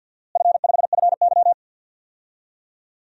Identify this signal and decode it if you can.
Morse code, heard as a keyed tone, about 46 wpm, 700 Hz. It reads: F5FY
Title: DXCC-100 Call Signs - 4 Characters - Repeated 2x - 50wpm